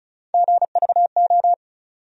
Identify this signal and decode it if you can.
Morse code, heard as a keyed tone, about 36 wpm, 700 Hz. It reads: GVO
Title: USA Call Sign Postfixes - 3 Letters 35wpm